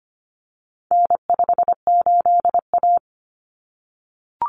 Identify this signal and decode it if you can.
Morse code, heard as a keyed tone, about 25 wpm, 700 Hz. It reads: N58A
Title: Common Cut Numbers - 4-Digit 25wpm